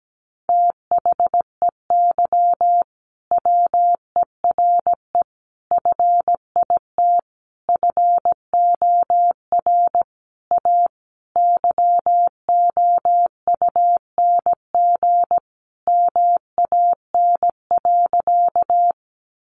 Morse code, heard as a keyed tone, about 17 wpm, 700 Hz. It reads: THEY WERE FIT FOR A YOUNG MAN.
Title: Sentences from Top 800 Words 17wpm